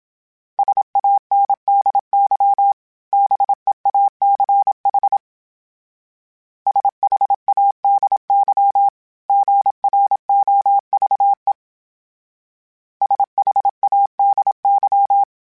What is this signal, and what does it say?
Morse code, heard as a keyed tone, about 27 wpm, 800 Hz. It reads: SANDY BEACH SHADY GROVE SHADY
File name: fromSMs / FromSMs mp3s / Cities@25